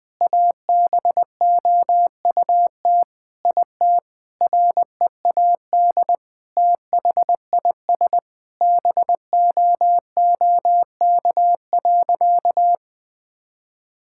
Morse code, heard as a keyed tone, about 21 wpm, 700 Hz. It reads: ABOUT IT READ THIS BOOK.
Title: Sentences with 7-Word Phrases - 20wpm